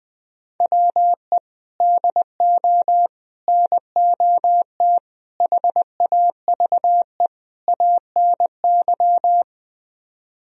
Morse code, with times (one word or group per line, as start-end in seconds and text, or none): 0.6-1.4 WE
1.8-3.1 DO
3.5-5.0 NOT
5.4-7.3 HAVE
7.7-9.4 ANY